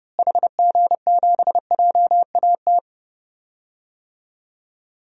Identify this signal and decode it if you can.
Morse code, heard as a keyed tone, about 30 wpm, 700 Hz. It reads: HG7JAT
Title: DXCC-100 Call Signs - 6 Characters - Repeated 2x - 30wpm